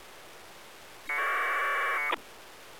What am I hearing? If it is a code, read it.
Packet radio (AX.25, as in APRS).